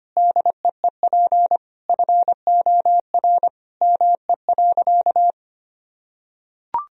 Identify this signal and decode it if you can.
Morse code, heard as a keyed tone, about 25 wpm, 700 Hz. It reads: DEEP FOR ME.
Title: Sentences from Top 1000 Words - Review Part 3 - 25wpm